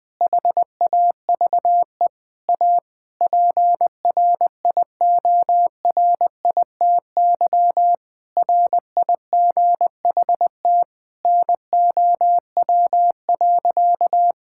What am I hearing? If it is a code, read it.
Morse code, heard as a keyed tone, about 20 wpm, 700 Hz. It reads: HAVE A PRIORITY RIGHT NOW.
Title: Sentences from Top 1600 Words 20wpm